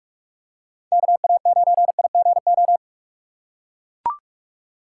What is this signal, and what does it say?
Morse code, heard as a keyed tone, about 44 wpm, 700 Hz. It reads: KA9IGO